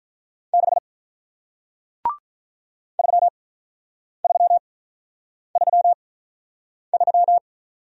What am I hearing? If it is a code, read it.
Morse code, heard as a keyed tone, about 48 wpm, spaced as for 9 wpm, 700 Hz. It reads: 6 3333